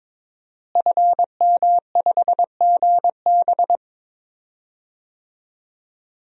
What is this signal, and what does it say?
Morse code, heard as a keyed tone, about 22 wpm, 700 Hz. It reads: FM5GB
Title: DXCC-100 Call Signs - 5 Characters - Repeated 3x - 22wpm